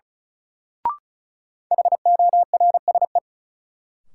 Morse code, heard as a keyed tone, about 36 wpm, 700 Hz. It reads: HORSE
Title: Sentences from Top 300 Words - New Word Drill - 35wpm